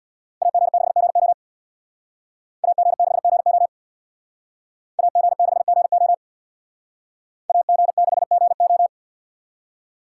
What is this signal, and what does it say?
Morse code, heard as a keyed tone, about 65 wpm, 700 Hz. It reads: AG6GO AG6GO AG6GO AG6GO